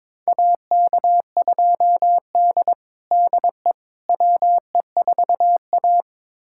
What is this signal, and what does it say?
Morse code, heard as a keyed tone, about 22 wpm, 700 Hz. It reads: AK2D DE WE4A
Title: USA Callsign DE Callsign - 4 Characters - 22wpm